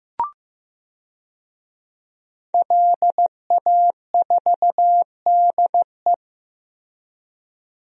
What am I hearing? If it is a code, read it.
Morse code, heard as a keyed tone, about 15 wpm, 700 Hz. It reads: LA4DE
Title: DXCC-100 Call Signs - 5 Characters - Repeated 2x - 15wpm